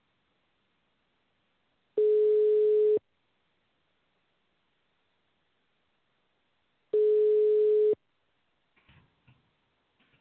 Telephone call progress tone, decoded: ringback tone